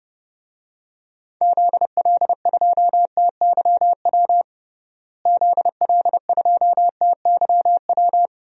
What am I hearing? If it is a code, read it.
Morse code, heard as a keyed tone, about 32 wpm, 700 Hz. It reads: ZL2TYW ZL2TYW
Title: DXCC-100 Call Signs - 6 Characters - Repeated 2x - 30wpm